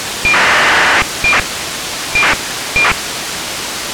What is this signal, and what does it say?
ACARS aircraft data.